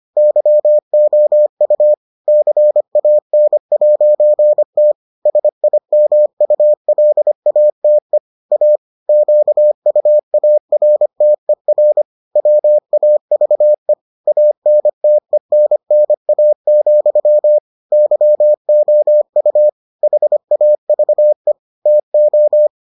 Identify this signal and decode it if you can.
Morse code, heard as a keyed tone, about 25 wpm, 600 Hz. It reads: YOU CAN'T SIMULATE A QUARTER WAVE ANTENNA, YOU HAVE TO